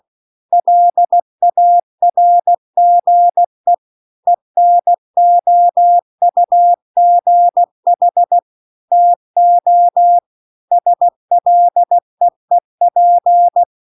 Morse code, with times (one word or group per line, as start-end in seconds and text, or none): 0.5-3.7 LARGE
4.3-8.4 ENOUGH
8.9-10.2 TO
10.7-13.6 SLEEP